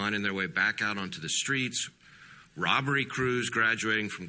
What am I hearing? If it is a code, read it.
Emergency Alert System (SAME) tones, heard as harsh buzzing bursts, with speech. It is heard from 0.3 to 4.0 s.
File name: wrct-log hi- 18